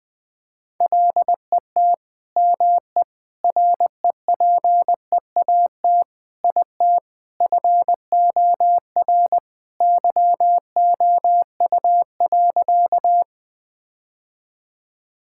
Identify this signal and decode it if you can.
Morse code, heard as a keyed tone, about 21 wpm, 700 Hz. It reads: LET ME REPEAT IT FOR YOU.
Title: Sentences from Top 1600 Words 20wpm